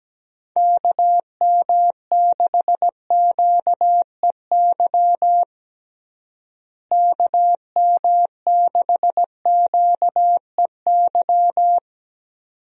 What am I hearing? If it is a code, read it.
Morse code, heard as a keyed tone, about 17 wpm, 700 Hz. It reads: KM6QEY KM6QEY
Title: USA Call Signs - 6 Characters - Repeated 2x - 17wpm